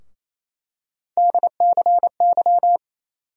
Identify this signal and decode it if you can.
Morse code, heard as a keyed tone, about 28 wpm, 700 Hz. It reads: DCY
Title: USA Call Sign Postfixes - 3 Letters 28wpm